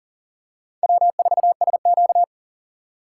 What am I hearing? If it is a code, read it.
Morse code, heard as a keyed tone, about 37 wpm, 700 Hz. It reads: W4SQ
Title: USA Call Signs - 4x Character Spacing - 40wpm